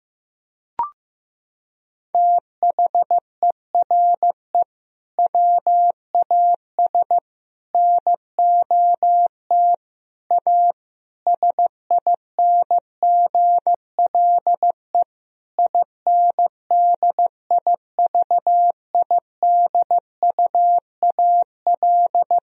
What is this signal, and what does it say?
Morse code, heard as a keyed tone, about 15 wpm, 700 Hz. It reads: THERE WAS NOT A SINGLE INDIVIDUAL
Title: Sentences from Top 1000 Words - Review Part 4 - 15wpm